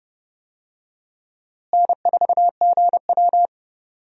Morse code, heard as a keyed tone, about 30 wpm, 700 Hz. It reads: N4GW
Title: USA Call Signs - 4 Characters - Repeated 2x - 30wpm